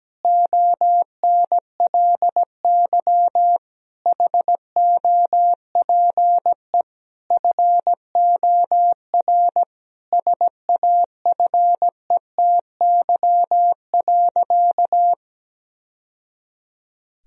Morse code, heard as a keyed tone, about 17 wpm, 700 Hz. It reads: ONLY HOPE FOR SAFETY.
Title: Sentences from Top 1000 Words - Review Part 4 - 17wpm